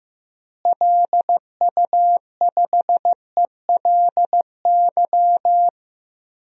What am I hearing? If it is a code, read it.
Morse code, heard as a keyed tone, about 15 wpm, 700 Hz. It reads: LU5ELY
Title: DXCC-100 Call Signs - 6 Characters - Repeated 2x - 15wpm